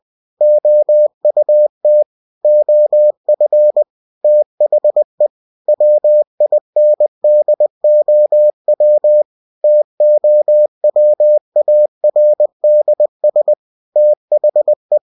Morse code, heard as a keyed tone, about 20 wpm, 600 Hz. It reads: OUT OF THE WINDOW TOWARDS THE
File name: cw-books / winnie-the-pooh / winnie-20wpm-page-0101